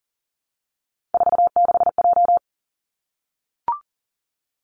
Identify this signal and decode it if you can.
Morse code, heard as a keyed tone, about 41 wpm, 700 Hz. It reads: 46J
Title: Three Challenging Letter-Number Combinations 40wpm